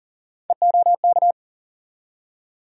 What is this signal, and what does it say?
Morse code, heard as a keyed tone, about 37 wpm, 700 Hz. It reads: EOK
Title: USA Call Sign Postfixes - 3 Letters 40wpm